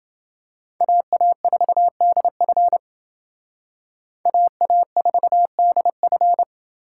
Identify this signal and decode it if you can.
Morse code, heard as a keyed tone, about 28 wpm, 700 Hz. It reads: AA4DF AA4DF